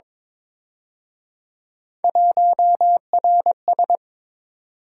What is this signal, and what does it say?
Morse code, heard as a keyed tone, about 26 wpm, 700 Hz. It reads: 1RS
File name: Speed-Racing - Three Letter-Number Combinations - 22wpm